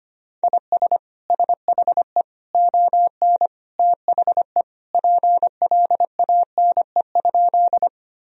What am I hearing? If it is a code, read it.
Morse code, heard as a keyed tone, about 25 wpm, 700 Hz. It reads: IS SHE ON THE PLANE?
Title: Sentences from Top 1000 Words - Review Part 3 - 25wpm